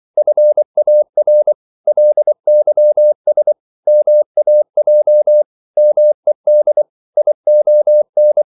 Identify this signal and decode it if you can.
Morse code, heard as a keyed tone, about 24 wpm, 600 Hz. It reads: FAR LYS MAJ MED ION
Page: Korte danske ord 24wpm